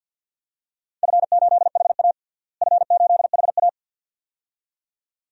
Morse code, heard as a keyed tone, about 47 wpm, 700 Hz. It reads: F8HA F8HA
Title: DXCC-100 Call Signs - 4 Characters - Repeated 2x - 50wpm